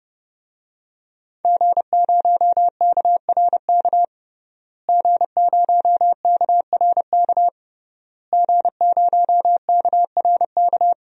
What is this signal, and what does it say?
Morse code, heard as a keyed tone, about 32 wpm, 700 Hz. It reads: G0KRK G0KRK G0KRK
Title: DXCC-100 Call Signs - 5 Characters - Repeated 3x - 30wpm